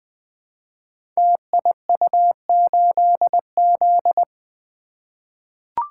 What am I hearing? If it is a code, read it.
Morse code, heard as a keyed tone, about 20 wpm, 700 Hz. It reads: TIU8Z
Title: Code Groups - 5 Letter:Numbers - 20wpm